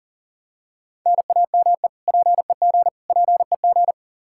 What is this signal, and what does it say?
Morse code, heard as a keyed tone, about 40 wpm, 700 Hz. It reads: NAME PEG PEG
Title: Practice QSO - Exchange 1 - Beginning - 40wpm